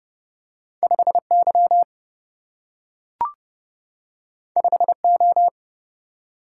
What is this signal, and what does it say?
Morse code, heard as a keyed tone, about 30 wpm, 700 Hz. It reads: 5 Y 5 O
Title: Single-Letter Single-Number 30wpm